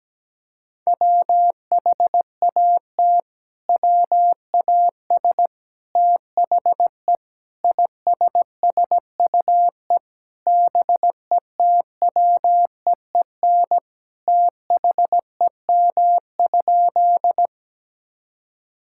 Morse code, heard as a keyed tone, about 17 wpm, 700 Hz. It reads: WHAT WAS THE ISSUE BETWEEN THEM?
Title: Sentences from Top 1000 Words - Review Part 6 - 17wpm